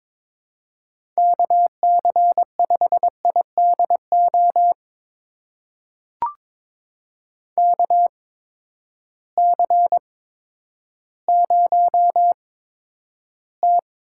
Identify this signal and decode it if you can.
Morse code, heard as a keyed tone, about 22 wpm, spaced as for 11 wpm, 700 Hz. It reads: KC5IDO KC0T
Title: USA Call Signs - 6 Characters - 8x Character Spacing - 22wpm